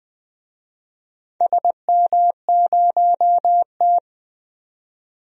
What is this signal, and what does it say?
Morse code, heard as a keyed tone, about 20 wpm, 700 Hz. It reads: SM0T
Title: DXCC-100 Call Signs - 4 Characters - 20wpm